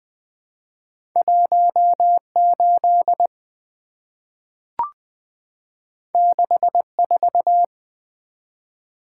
Morse code, heard as a keyed tone, about 20 wpm, 700 Hz. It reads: 18 64
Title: Two Letter-Number Combinations - 20wpm